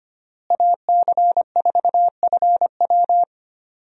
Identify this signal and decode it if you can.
Morse code, heard as a keyed tone, about 25 wpm, 700 Hz. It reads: AC4FW
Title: USA Call Signs - 5 Characters - Repeated 3x - 25wpm